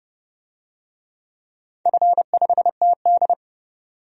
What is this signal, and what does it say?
Morse code, heard as a keyed tone, about 30 wpm, 700 Hz. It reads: F5TD
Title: DXCC-100 Call Signs - 4 Characters - 30wpm